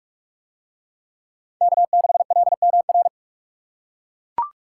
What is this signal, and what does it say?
Morse code, heard as a keyed tone, about 41 wpm, 700 Hz. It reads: K6LMR